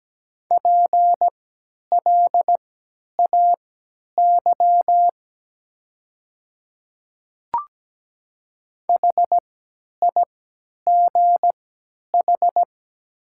Morse code, heard as a keyed tone, about 17 wpm, spaced as for 10 wpm, 700 Hz. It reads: PLAY HIGH
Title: 3x Character Spacing - Top 200 Words - 17wpm